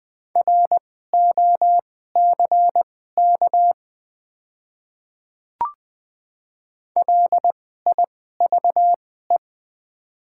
Morse code, heard as a keyed tone, about 20 wpm, spaced as for 14 wpm, 690 Hz. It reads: ROCK LIVE
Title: 2x Character Spacing - Top 500 Words - 20wpm